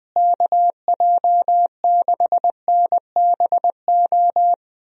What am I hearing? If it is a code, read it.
Morse code, heard as a keyed tone, about 20 wpm, 700 Hz. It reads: KJ6NBO
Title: USA Call Signs - 6 Characters - Repeated 2x - 20wpm